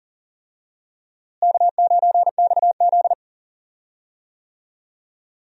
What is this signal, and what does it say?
Morse code, heard as a keyed tone, about 40 wpm, 700 Hz. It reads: K9XZ
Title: USA Call Signs - 4 Characters - Repeated 3x - 40wpm